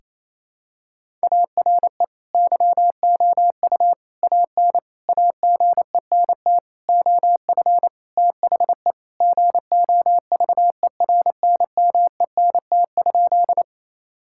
Morse code, heard as a keyed tone, about 27 wpm, 700 Hz. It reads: ARE YOU AN AGENT OF THE GOVERNMENT?
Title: Sentences from Top 1600 Words 28wpm